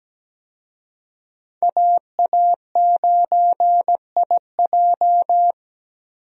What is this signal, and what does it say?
Morse code, heard as a keyed tone, about 17 wpm, 700 Hz. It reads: AA9IJ
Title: USA Call Signs - 5 Characters - Repeated 2x - 17wpm